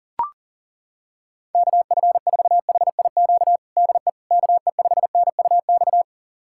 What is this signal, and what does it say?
Morse code, heard as a keyed tone, about 40 wpm, 700 Hz. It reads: KF4HIQ DE KE5NUX
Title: USA Callsign DE Callsign 40wpm